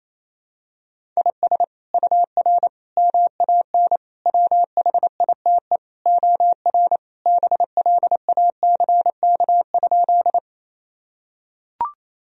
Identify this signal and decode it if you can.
Morse code, heard as a keyed tone, about 28 wpm, 700 Hz. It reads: IS UR MAN WHITE OR BLACK?
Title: Sentences from Top 400 Words and Abbreviations 28wpm